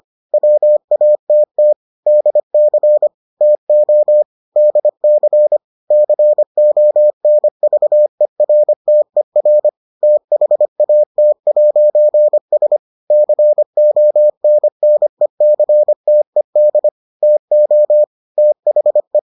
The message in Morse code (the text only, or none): WATT DC TO DC CONVERTER THAT'S CONNECTED TO THE